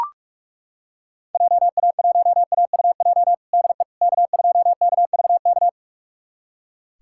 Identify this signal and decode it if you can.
Morse code, heard as a keyed tone, about 43 wpm, 690 Hz. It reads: JA1AUJ DE K2KVK
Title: USA Callsign DE Callsign 45wpm